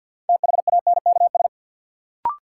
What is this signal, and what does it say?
Morse code, heard as a keyed tone, about 53 wpm, 700 Hz. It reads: THANKS
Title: Sets of 5 Words - New Word Drill - 50wpm